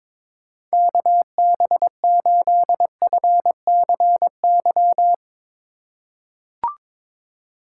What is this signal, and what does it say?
Morse code, heard as a keyed tone, about 22 wpm, 700 Hz. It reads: KB8FCY
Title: USA Call Signs - 6 Characters - 3x Character Spacing - 22wpm